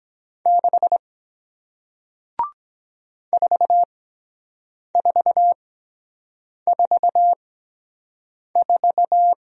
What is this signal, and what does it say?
Morse code, heard as a keyed tone, about 23 wpm, spaced as for 7 wpm, 700 Hz. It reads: 6 4444